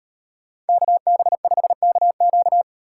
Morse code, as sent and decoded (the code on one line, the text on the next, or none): -.- -... ..... -.- --.-
KB5KQ